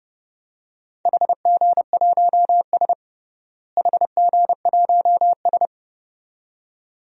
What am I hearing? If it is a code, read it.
Morse code, heard as a keyed tone, about 32 wpm, 700 Hz. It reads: HG1S HG1S
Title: DXCC-100 Call Signs - 4 Characters - Repeated 2x - 30wpm